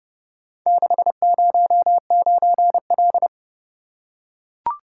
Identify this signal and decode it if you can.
Morse code, heard as a keyed tone, about 30 wpm, 700 Hz. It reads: 609L